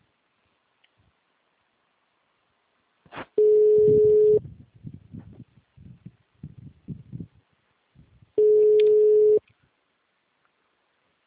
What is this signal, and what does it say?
Ringback tone (Europe, 425 Hz).